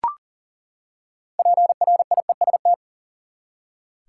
Morse code, heard as a keyed tone, about 40 wpm, 700 Hz. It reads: PRIEST
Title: Six Letter Words - 40wpm